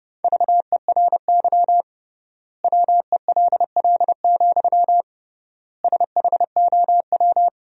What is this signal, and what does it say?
Morse code, heard as a keyed tone, about 32 wpm, 700 Hz. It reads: VERY WELL, SHOW
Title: 3x Word-Spacing - Sentences from Top 200 Words - 30wpm